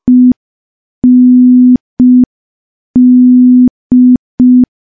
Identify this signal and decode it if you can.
Morse code, heard as a keyed tone, about 5 wpm, 250 Hz. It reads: END
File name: morse / morse-training-05WPM